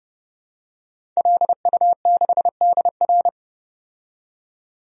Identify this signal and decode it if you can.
Morse code, heard as a keyed tone, about 29 wpm, 700 Hz. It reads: LU6DR
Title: DXCC-100 Call Signs - 5 Characters - 30wpm